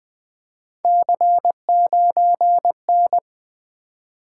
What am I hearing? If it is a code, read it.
Morse code, heard as a keyed tone, about 20 wpm, 700 Hz. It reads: C9N
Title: Three Letter-Number Combinations - 20wpm